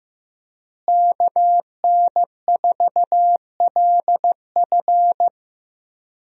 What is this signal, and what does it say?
Morse code, heard as a keyed tone, about 15 wpm, 700 Hz. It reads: KN4LF
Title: USA Call Signs - 5 Characters - Repeated 2x - 15wpm